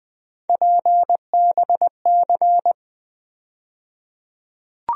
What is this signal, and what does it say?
Morse code, heard as a keyed tone, about 20 wpm, 700 Hz. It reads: PBC
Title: USA Call Sign Postfixes - 3 Letters 20wpm